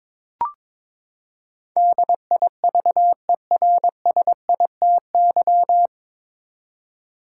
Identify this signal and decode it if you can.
Morse code, heard as a keyed tone, about 21 wpm, 700 Hz. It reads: DIVERSITY
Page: Morse Code Ninja practice 22wpm